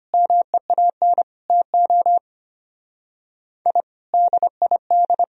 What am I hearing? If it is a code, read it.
Morse code, heard as a keyed tone, about 29 wpm, 700 Hz. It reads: MEAN TO I DID
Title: Speed-Racing - Sets of 5 Words 25wpm